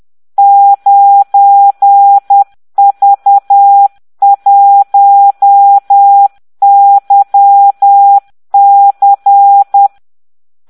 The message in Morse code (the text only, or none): 9V1YC